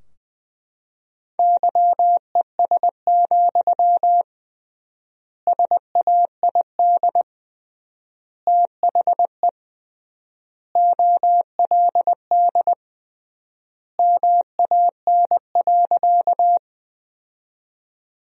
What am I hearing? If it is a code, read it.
Morse code, heard as a keyed tone, about 20 wpm, 700 Hz. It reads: YES, SAID THE OLD MAN.
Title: 3x Word-Spacing - Sentences from Top 500 Words - 20wpm